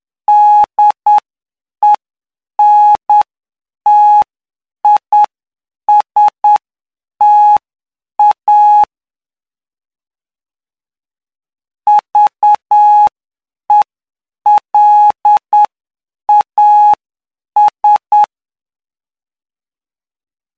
Morse code, heard as a keyed tone, about 9 wpm, spaced as for 7 wpm, 840 Hz. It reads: DENTISTA VELAS
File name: cw / py2lcd / 07-PY2LCD-Li��o 16